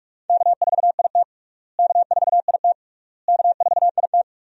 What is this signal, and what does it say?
Morse code, heard as a keyed tone, about 46 wpm, 700 Hz. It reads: K4IT K4IT K4IT
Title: USA Call Signs - 4 Characters - Repeated 3x - 45wpm